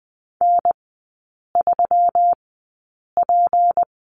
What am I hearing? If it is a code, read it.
Morse code, heard as a keyed tone, about 20 wpm, spaced as for 8 wpm, 690 Hz. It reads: N3P
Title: 166 Lesson 27e - Number 7 - Three Characters 20wpm